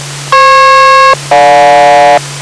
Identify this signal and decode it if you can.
Aircraft SELCAL, heard as two pairs of tones. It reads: FP-GJ